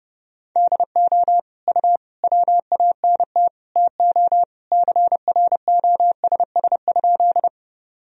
Morse code, heard as a keyed tone, about 30 wpm, 700 Hz. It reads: DO U WANT TO CROSS?
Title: Sentences from Top 300 Words and Abbreviations 30wpm